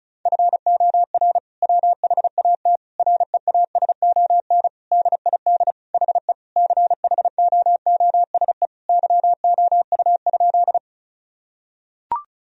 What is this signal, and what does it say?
Morse code, heard as a keyed tone, about 34 wpm, 700 Hz. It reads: FOR WHAT REASON DID HE CHOOSE YOU?
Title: Sentences from Top 1000 Words - Review Part 4 - 35wpm